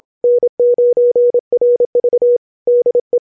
Morse code, heard as a keyed tone, about 27 wpm, 490 Hz. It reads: N9RV DE